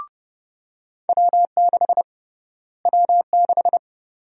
Morse code, heard as a keyed tone, about 29 wpm, 700 Hz. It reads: W6 W6
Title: USA Call Sign Prefixes - Repeated 2x - 30wpm